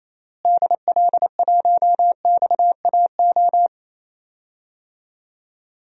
Morse code, heard as a keyed tone, about 28 wpm, 700 Hz. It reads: DL1XAO
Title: DXCC-100 Call Signs - 6 Characters - Repeated 2x - 28wpm